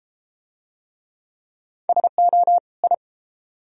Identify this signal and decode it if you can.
Morse code, heard as a keyed tone, about 31 wpm, 700 Hz. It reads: SO I